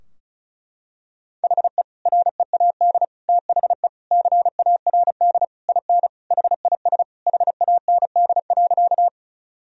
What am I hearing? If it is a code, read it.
Morse code, heard as a keyed tone, about 34 wpm, 700 Hz. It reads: HE READ THE CARD IN HIS HAND.
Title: Sentences from Top 1000 Words - Review Part 4 - 35wpm